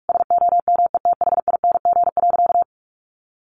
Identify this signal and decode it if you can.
Morse code, heard as a keyed tone, about 44 wpm, 700 Hz. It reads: SOMETHING.
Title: Sentences from Top 1200 Words 45wpm